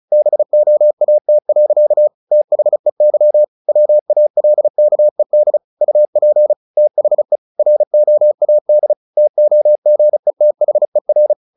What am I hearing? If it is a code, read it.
Morse code, heard as a keyed tone, about 34 wpm, 600 Hz. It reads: BOAT. THEY WALKED UP THE ROAD TOGETHER
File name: the old man and the sea / the old man and the sea 1 35wpm